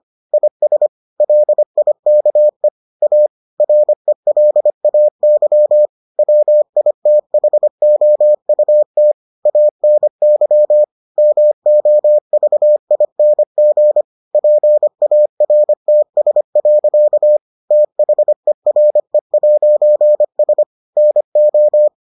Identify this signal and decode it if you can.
Morse code, heard as a keyed tone, about 25 wpm, 600 Hz. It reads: IS LIKE A RELAY WITHOUT ANY MOVING PARTS. THERE'S NO